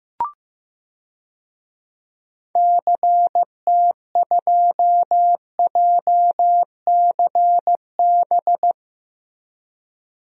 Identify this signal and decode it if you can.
Morse code, heard as a keyed tone, about 16 wpm, 700 Hz. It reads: CT2JCB
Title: DXCC-100 Call Signs - 6 Characters - Repeated 2x - 15wpm